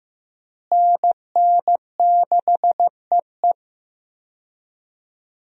Morse code, heard as a keyed tone, about 15 wpm, 700 Hz. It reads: NN6EE
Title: USA Call Signs - 5 Characters - Repeated 2x - 15wpm